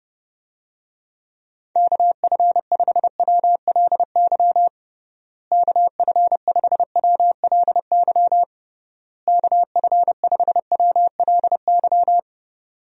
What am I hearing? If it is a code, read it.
Morse code, heard as a keyed tone, about 30 wpm, 700 Hz. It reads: KF5WLY KF5WLY KF5WLY
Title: USA Call Signs - 6 Characters - Repeated 3x - 30wpm